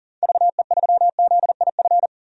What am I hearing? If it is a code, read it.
Morse code, heard as a keyed tone, about 40 wpm, 700 Hz. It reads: VE3ZIF
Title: DXCC-100 Call Signs - 6 Characters - 40wpm